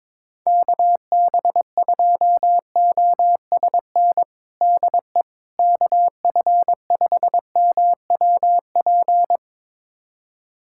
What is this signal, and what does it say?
Morse code, heard as a keyed tone, about 22 wpm, 700 Hz. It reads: KB2OSN DE KF5MWP
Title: USA Callsign DE Callsign 22wpm